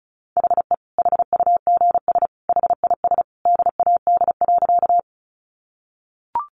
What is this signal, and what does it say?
Morse code, heard as a keyed tone, about 34 wpm, 700 Hz. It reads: HE HUGS HIS DAD.
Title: Easy Phrases 35wpm